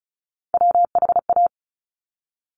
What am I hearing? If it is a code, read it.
Morse code, heard as a keyed tone, about 36 wpm, 700 Hz. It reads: WHA
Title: Top Three Letter Combinations in Words 35wpm